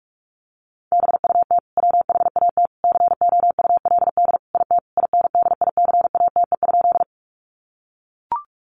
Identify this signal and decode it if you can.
Morse code, heard as a keyed tone, about 44 wpm, 700 Hz. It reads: BUT WHAT COULD IT INDICATE?
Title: Sentences from Top 1300 Words 45wpm